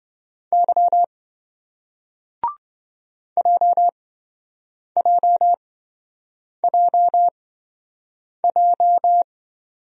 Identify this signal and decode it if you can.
Morse code, heard as a keyed tone, about 26 wpm, spaced as for 7 wpm, 700 Hz. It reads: Y JJJJ